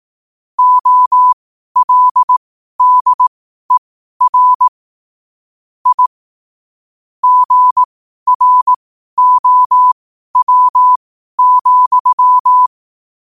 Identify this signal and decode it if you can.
Morse code, heard as a keyed tone, about 18 wpm, spaced as for 12 wpm, 1000 Hz. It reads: OLDER I GROW,
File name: qotdfiles / 12wpm